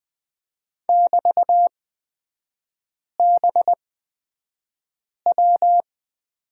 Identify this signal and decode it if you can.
Morse code, heard as a keyed tone, about 20 wpm, spaced as for 5 wpm, 700 Hz. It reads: <BT>BW
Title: 243 Lesson 39e - Letter z - Three Characters 20wpm